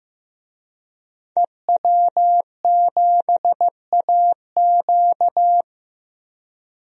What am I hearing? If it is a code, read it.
Morse code, heard as a keyed tone, about 15 wpm, 700 Hz. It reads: EW7AQ
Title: DXCC-100 Call Signs - 5 Characters - 15wpm